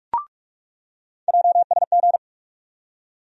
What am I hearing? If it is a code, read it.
Morse code, heard as a keyed tone, about 43 wpm, 700 Hz. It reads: JSG
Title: USA Call Sign Postfixes - 3 Letters 45wpm